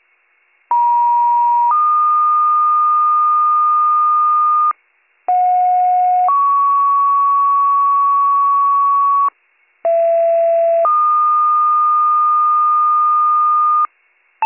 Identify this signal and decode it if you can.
Two-tone page, heard as a one-second tone, then a three-second tone, about 950 then 1220 Hz, about 730 then 1120 Hz, about 670 then 1220 Hz. It is heard from 0.7 to 13.9 s.